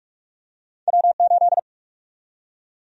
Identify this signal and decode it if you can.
Morse code, heard as a keyed tone, about 46 wpm, 700 Hz. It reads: W8
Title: USA Call Sign Prefixes - 45wpm